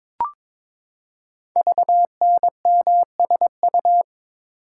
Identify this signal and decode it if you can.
Morse code, heard as a keyed tone, about 23 wpm, 700 Hz. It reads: VNMSU
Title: Code Groups - 5 Letters - 22wpm